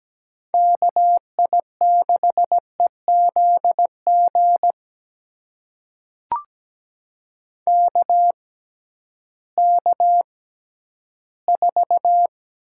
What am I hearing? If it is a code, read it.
Morse code, heard as a keyed tone, about 17 wpm, spaced as for 11 wpm, 700 Hz. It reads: KI6EZG KK4
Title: USA Call Signs - 6 Characters - 6x Character Spacing - 17wpm